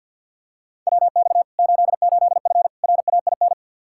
Morse code, heard as a keyed tone, about 47 wpm, 700 Hz. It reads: WX 88F RAIN
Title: Practice QSO - Exchange 2 - Info - 50wpm